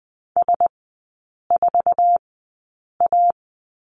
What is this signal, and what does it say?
Morse code, heard as a keyed tone, about 20 wpm, spaced as for 8 wpm, 700 Hz. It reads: S4A
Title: 047 Lesson 9e - Number 4 - Three Characters 20wpm